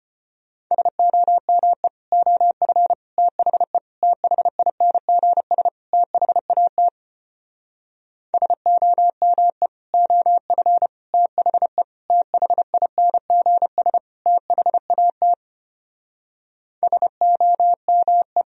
Morse code, heard as a keyed tone, about 33 wpm, 700 Hz. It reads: SOME OF THE THINGS THAT SOME OF THE THINGS THAT SOME